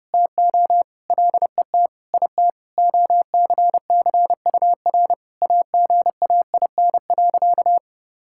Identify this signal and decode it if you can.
Morse code, heard as a keyed tone, about 30 wpm, 700 Hz. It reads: TO LET IT OCCUR AGAIN.
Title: Sentences from Top 1000 Words - Review Part 6 - 30wpm